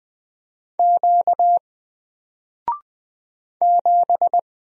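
Morse code, heard as a keyed tone, about 20 wpm, 700 Hz. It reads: Q 7
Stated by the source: Single letter and number Morse drill at 20 wpm, from N4LCD.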